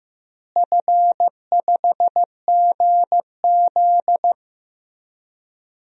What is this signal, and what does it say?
Morse code, heard as a keyed tone, about 15 wpm, 700 Hz. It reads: F5GZ
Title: DXCC-100 Call Signs - 4 Characters - Repeated 3x - 15wpm